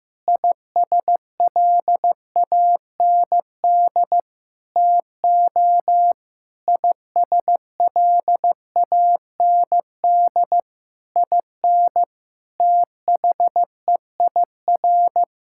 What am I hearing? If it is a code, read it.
Morse code, heard as a keyed tone, about 16 wpm, 700 Hz. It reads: ISLAND TO ISLAND IN THEIR
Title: Sentences from Top 1000 Words - Review Part 6 - 15wpm